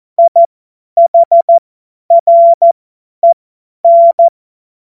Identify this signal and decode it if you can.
Morse code, heard as a keyed tone, about 14 wpm, spaced as for 10 wpm, 680 Hz. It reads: IHREN
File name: Morsekurs / Witze / Witz 16. Dezember 2025 50 F70